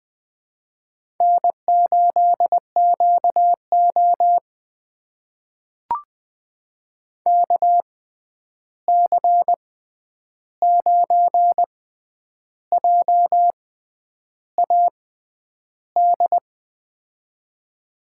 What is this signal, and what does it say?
Morse code, heard as a keyed tone, about 20 wpm, spaced as for 9 wpm, 700 Hz. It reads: N8QO KC9JAD
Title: USA Call Signs - 6x Character Spacing - 20wpm